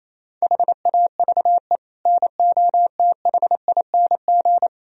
Morse code, heard as a keyed tone, about 36 wpm, 700 Hz. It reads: HAVE NOTHING